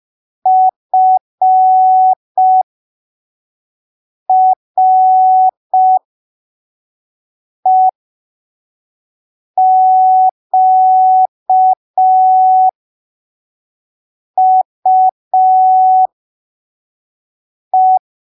Morse code, heard as a keyed tone, about 5 wpm, spaced as for 3 wpm, 750 Hz. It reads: FREQUE
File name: arquivos / texto / texto 06 05ppm